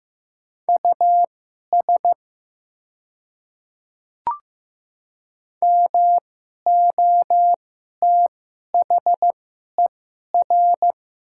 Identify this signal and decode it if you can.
Morse code, heard as a keyed tone, about 15 wpm, spaced as for 11 wpm, 700 Hz. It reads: US MOTHER
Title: 2x Character Spacing - Top 300 Words - 15wpm